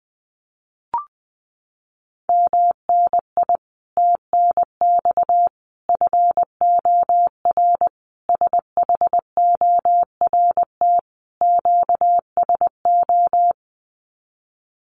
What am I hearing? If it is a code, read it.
Morse code, heard as a keyed tone, about 20 wpm, 700 Hz. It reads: MNI TNX FOR SHORT QSO
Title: Common QSO Phrases 20wpm